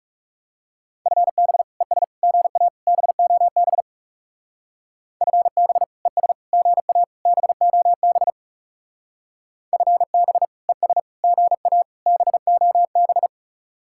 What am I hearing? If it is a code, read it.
Morse code, heard as a keyed tone, about 36 wpm, 700 Hz. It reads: FB ES GA BOB FB ES GA BOB FB ES GA BOB
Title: Speed-Racing Common QSO Phrases - 30wpm